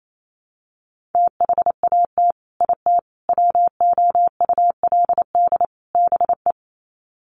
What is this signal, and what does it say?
Morse code, heard as a keyed tone, about 28 wpm, 700 Hz. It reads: THAT IT WOULD BE
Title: Sets of 4 Words 28wpm